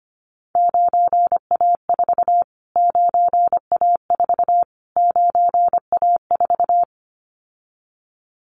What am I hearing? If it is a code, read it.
Morse code, heard as a keyed tone, about 25 wpm, 700 Hz. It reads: 9A4 9A4 9A4
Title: Callsign Prefixes for Top 25 DXCC - Repeated 3x - 25wpm